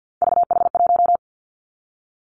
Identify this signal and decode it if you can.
Morse code, heard as a keyed tone, about 50 wpm, 700 Hz. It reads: VH1
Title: Three Challenging Letter-Number Combinations 50wpm